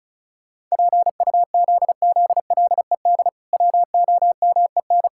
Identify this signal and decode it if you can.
Morse code, heard as a keyed tone, about 34 wpm, 700 Hz. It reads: PUZZLED WOMEN
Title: Pangram Challenge v2 - 35wpm